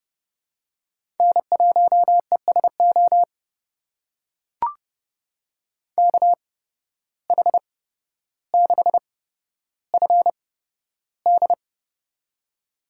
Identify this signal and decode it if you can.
Morse code, heard as a keyed tone, about 29 wpm, spaced as for 13 wpm, 700 Hz. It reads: N1ESO KH6FD
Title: USA Call Signs - 5 Characters - 8x Character Spacing - 30wpm